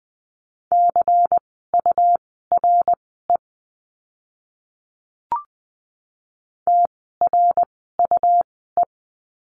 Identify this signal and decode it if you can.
Morse code, heard as a keyed tone, about 20 wpm, spaced as for 14 wpm, 700 Hz. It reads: CURE TRUE
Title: 101 Lesson 17f - Letter u - Words 20wpm